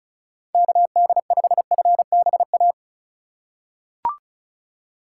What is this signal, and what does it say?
Morse code, heard as a keyed tone, about 34 wpm, 700 Hz. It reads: KD5FBA
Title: K1USN SST QSO Practice - 35wpm